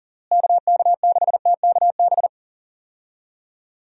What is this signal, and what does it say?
Morse code, heard as a keyed tone, about 37 wpm, 700 Hz. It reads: KK6TKB
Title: USA Call Signs - 6 Characters - Repeated 3x - 40wpm